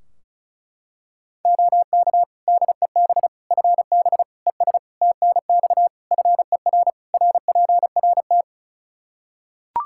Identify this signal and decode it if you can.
Morse code, heard as a keyed tone, about 36 wpm, 700 Hz. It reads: OK DEB FB ES TNX FER RPRT
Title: Practice QSO - Exchange 2 - Info - 35wpm